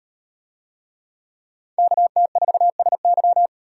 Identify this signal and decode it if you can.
Morse code, heard as a keyed tone, about 38 wpm, 700 Hz. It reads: KT4SY